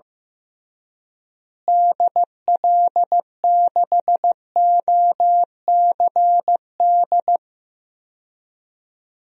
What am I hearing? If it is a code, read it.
Morse code, heard as a keyed tone, about 15 wpm, 700 Hz. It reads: DL6OCD
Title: DXCC-100 Call Signs - 6 Characters - Repeated 3x - 15wpm